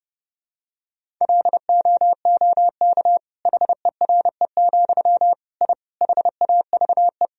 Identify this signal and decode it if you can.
Morse code, heard as a keyed tone, about 30 wpm, 700 Hz. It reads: LOOK HERE, I HAVE
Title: Sentences from Top 1000 Words - Review Part 1 - 30wpm